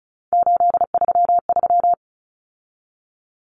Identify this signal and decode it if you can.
Morse code, heard as a keyed tone, about 34 wpm, 700 Hz. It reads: 833
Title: Common Cut Numbers - 3-Digit 35wpm